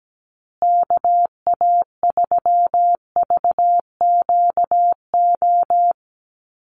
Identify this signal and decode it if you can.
Morse code, heard as a keyed tone, about 17 wpm, 700 Hz. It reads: KA3VQO
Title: USA Call Signs Repeated 2x 17wpm